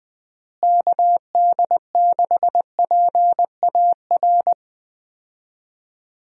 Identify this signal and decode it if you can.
Morse code, heard as a keyed tone, about 20 wpm, 700 Hz. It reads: KD6PAR
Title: USA Call Signs - 6 Characters - Repeated 3x - 20wpm